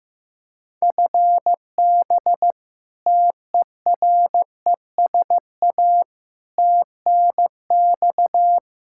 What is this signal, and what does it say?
Morse code, heard as a keyed tone, about 15 wpm, 700 Hz. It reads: FB TERESA TNX
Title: Practice QSO - Exchange 3 - More Info - 15wpm